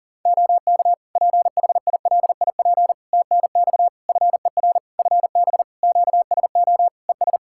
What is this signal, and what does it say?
Morse code, heard as a keyed tone, about 40 wpm, 700 Hz. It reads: OK PHILIP TNX FER FB QSO ES
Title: Practice QSO - Exchange 4 - Ending - 40wpm